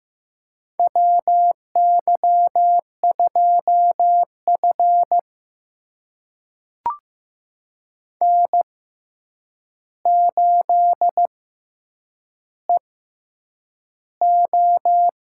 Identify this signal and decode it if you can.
Morse code, heard as a keyed tone, about 15 wpm, spaced as for 8 wpm, 700 Hz. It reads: WY2F N8EO
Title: USA Call Signs - 4 Characters - 6x Character Spacing - 15wpm